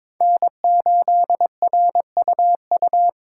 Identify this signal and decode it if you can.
Morse code, heard as a keyed tone, about 21 wpm, 700 Hz. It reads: N8RUU
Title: USA Call Signs - 5 Characters - 4x Character Spacing - 22wpm